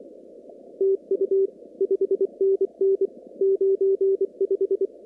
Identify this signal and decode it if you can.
Morse code, heard as a keyed tone, about 28 wpm, 390 Hz. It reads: TU 5NN 95
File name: CW / kt5j wpx cw-op-k5pi